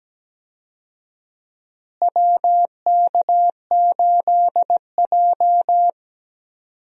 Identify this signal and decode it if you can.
Morse code, heard as a keyed tone, about 17 wpm, 700 Hz. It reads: WK8J
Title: USA Call Signs - 4 Characters - Repeated 2x - 17wpm